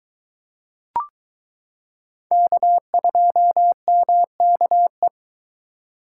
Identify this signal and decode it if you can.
Morse code, heard as a keyed tone, about 22 wpm, 700 Hz. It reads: K2MKE